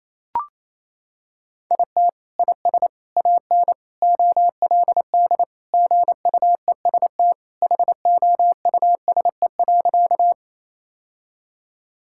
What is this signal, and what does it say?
Morse code, heard as a keyed tone, about 28 wpm, 700 Hz. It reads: IT IS AN OLD GUEST HOUSE.
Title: Sentences from Top 1600 Words 28wpm